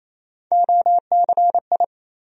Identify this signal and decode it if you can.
Morse code, heard as a keyed tone, about 28 wpm, 700 Hz. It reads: OCI
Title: USA Call Sign Postfixes - 3 Letters 28wpm